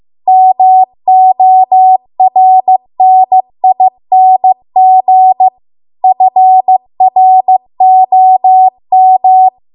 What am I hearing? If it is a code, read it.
Morse code, heard as a keyed tone, about 15 wpm, 750 Hz. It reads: MORNING FROM
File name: Archive / 15 WPM / 15WPM